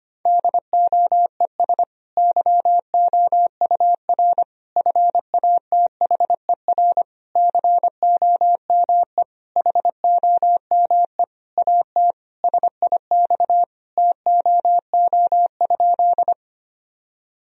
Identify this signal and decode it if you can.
Morse code, heard as a keyed tone, about 25 wpm, 700 Hz. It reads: DOES YOUR FATHER COME HOME AT SIX TOO?
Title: Sentences from Top 1000 Words - Review Part 2 - 25wpm